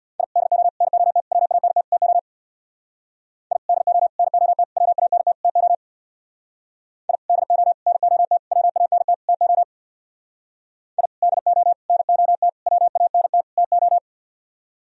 Morse code, heard as a keyed tone, about 63 wpm, 700 Hz. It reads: I DO NOT WANT TO I DO NOT WANT TO I DO NOT WANT TO I DO NOT WANT TO